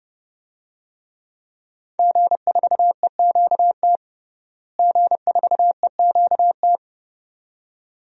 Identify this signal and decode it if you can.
Morse code, heard as a keyed tone, about 30 wpm, 700 Hz. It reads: G4EQT G4EQT
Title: DXCC-100 Call Signs - 5 Characters - Repeated 2x - 30wpm